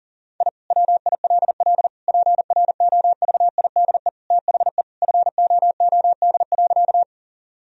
Morse code, heard as a keyed tone, about 40 wpm, 700 Hz. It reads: I WILL PROVIDE THE FOOD.
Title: Sentences from Top 1000 Words - Review Part 3 - 40wpm